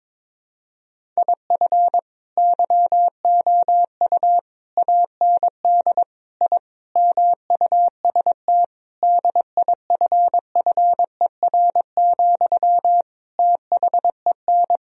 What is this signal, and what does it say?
Morse code, heard as a keyed tone, about 22 wpm, 700 Hz. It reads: IF YOU AND I MUST DIFFER, THEN
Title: Sentences from Top 1000 Words - Review Part 1 - 22wpm